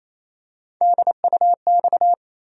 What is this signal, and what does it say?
Morse code, heard as a keyed tone, about 28 wpm, 700 Hz. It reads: DUX
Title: USA Call Sign Postfixes - 3 Letters 28wpm